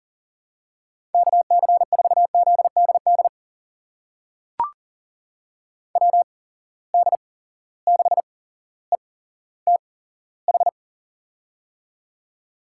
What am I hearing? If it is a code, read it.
Morse code, heard as a keyed tone, about 41 wpm, spaced as for 17 wpm, 700 Hz. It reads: KC4ZDD WD6ETH
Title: USA Call Signs - 6 Characters - 8x Character Spacing - 40wpm